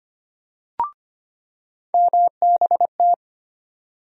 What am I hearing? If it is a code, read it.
Morse code, heard as a keyed tone, about 25 wpm, 700 Hz. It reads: MBT
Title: USA Call Sign Postfixes - 3 Letters 25wpm